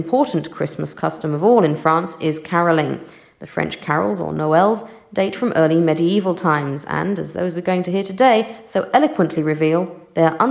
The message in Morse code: none